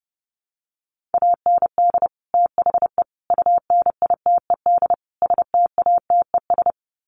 Morse code, heard as a keyed tone, about 30 wpm, 700 Hz. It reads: AND THE UNITED STATES
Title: Sets of 4 Words 30wpm